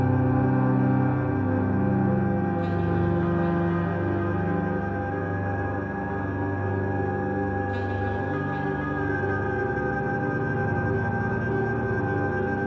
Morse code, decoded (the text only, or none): none